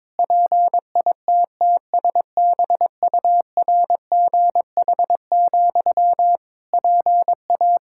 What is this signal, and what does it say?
Morse code, heard as a keyed tone, about 22 wpm, 700 Hz. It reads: PITTSBURGH, PA
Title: Practice QSO - Exchange 1 - Beginning - 22wpm